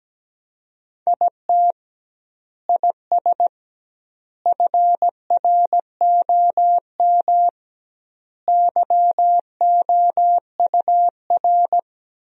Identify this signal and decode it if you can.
Morse code, heard as a keyed tone, about 17 wpm, 700 Hz. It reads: IT IS FROM YOUR
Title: 2x Word-Spacing - Sentences from Top 300 Words - 17wpm